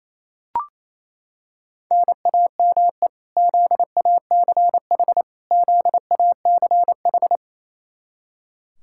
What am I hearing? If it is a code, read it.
Morse code, heard as a keyed tone, about 28 wpm, 700 Hz. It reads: NAME ZACH ZACH
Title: Practice QSO - Exchange 1 - Beginning - 28wpm